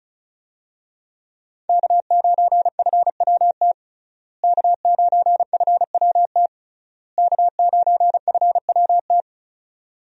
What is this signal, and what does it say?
Morse code, heard as a keyed tone, about 36 wpm, 700 Hz. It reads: K9FWT K9FWT K9FWT
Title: USA Call Signs - 5 Characters - Repeated 3x - 35wpm